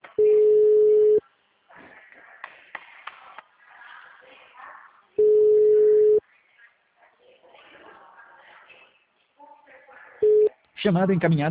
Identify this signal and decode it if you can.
Ringback tone (Europe, 425 Hz), with speech.